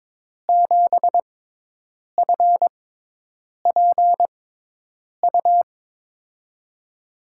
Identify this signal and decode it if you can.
Morse code, heard as a keyed tone, about 22 wpm, spaced as for 8 wpm, 700 Hz. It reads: 7FPU
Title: USA Call Signs - 6 Characters - 6x Character Spacing - 22wpm